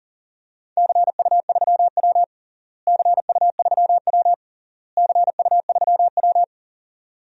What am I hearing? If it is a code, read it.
Morse code, heard as a keyed tone, about 40 wpm, 700 Hz. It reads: CU3W CU3W CU3W
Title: DXCC-100 Call Signs - 4 Characters - Repeated 3x - 40wpm